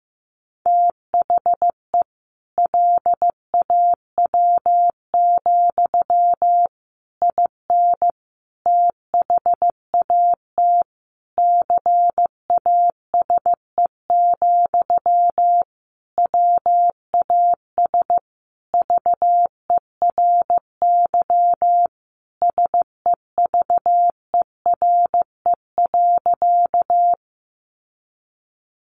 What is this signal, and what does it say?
Morse code, heard as a keyed tone, about 15 wpm, 700 Hz. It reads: THE LAW, IN THAT CASE, WAS VERY SEVERE.
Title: Sentences from Top 1400 Words 15wpm